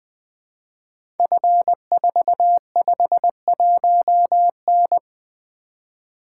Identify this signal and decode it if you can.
Morse code, heard as a keyed tone, about 20 wpm, 700 Hz. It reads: F451N
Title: Code Groups - 5 Letter:Numbers - 20wpm